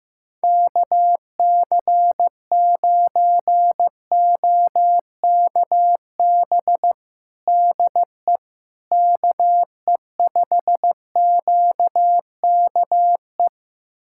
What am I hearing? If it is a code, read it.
Morse code, heard as a keyed tone, about 15 wpm, 700 Hz. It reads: KC9OKB DE KE5QKE
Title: USA Callsign DE Callsign - 6 Characters - 15wpm